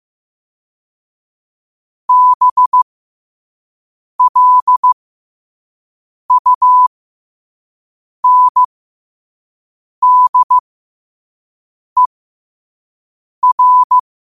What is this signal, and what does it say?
Morse code, heard as a keyed tone, about 15 wpm, spaced as for 5 wpm, 1000 Hz. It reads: BLUNDER
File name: qotdfiles / 5wpm